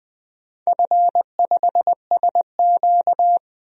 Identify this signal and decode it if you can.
Morse code, heard as a keyed tone, about 20 wpm, 700 Hz. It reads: F5SQ
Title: DXCC-100 Call Signs - 4 Characters - 20wpm